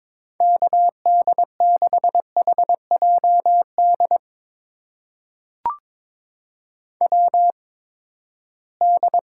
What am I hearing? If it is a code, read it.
Morse code, heard as a keyed tone, about 22 wpm, spaced as for 15 wpm, 700 Hz. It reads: KD6HJD WD
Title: USA Call Signs - 6 Characters - 8x Character Spacing - 22wpm